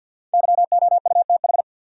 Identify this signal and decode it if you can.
Morse code, heard as a keyed tone, about 53 wpm, 700 Hz. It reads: YOUTH
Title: Five Letter Words - 50wpm